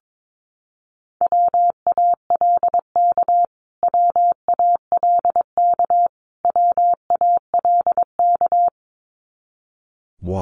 Morse code, heard as a keyed tone, about 22 wpm, 700 Hz. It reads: WALK WALK WALK
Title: Top 500 Words Repeated 22wpm